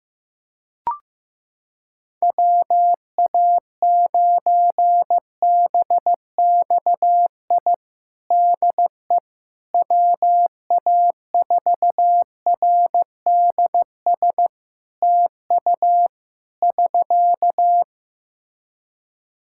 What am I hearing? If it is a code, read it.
Morse code, heard as a keyed tone, about 15 wpm, 700 Hz. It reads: WA9BXI DE WA4RDS TU <SK>
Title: Practice QSO - Exchange 4 - Ending - 15wpm